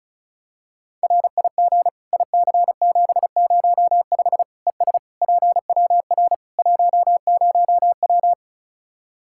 Morse code, heard as a keyed tone, about 36 wpm, 700 Hz. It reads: RIG IC705 ES PWR 10W
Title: Practice QSO - Exchange 2 - Info - 35wpm